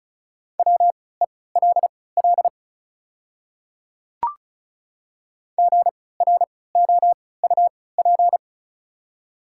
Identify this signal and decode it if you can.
Morse code, heard as a keyed tone, about 36 wpm, spaced as for 20 wpm, 690 Hz. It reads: WELL GROUP
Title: 3x Character Spacing - Top 500 Words - 35wpm